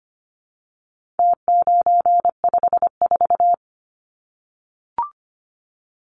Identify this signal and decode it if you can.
Morse code, heard as a keyed tone, about 25 wpm, 700 Hz. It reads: T954
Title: Numbers - Contest Style - 4-Digit 25wpm